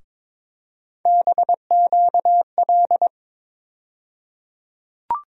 Morse code, heard as a keyed tone, about 22 wpm, 700 Hz. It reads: BQL
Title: USA Call Sign Postfixes - 3 Letters 22wpm